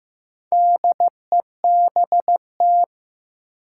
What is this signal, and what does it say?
Morse code, heard as a keyed tone, about 15 wpm, 700 Hz. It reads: DEBT
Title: Sentences from Top 1300 Words - New Word Drill Alt - 15wpm